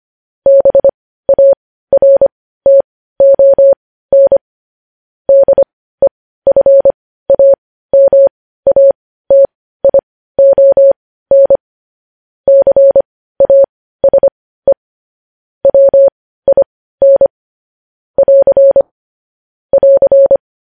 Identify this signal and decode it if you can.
Morse code, heard as a keyed tone, about 25 wpm, spaced as for 15 wpm, 550 Hz. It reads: BARTON DEFAMATION CASE WIN <AR> <AR>
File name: podcast / News headlines-1x-25x15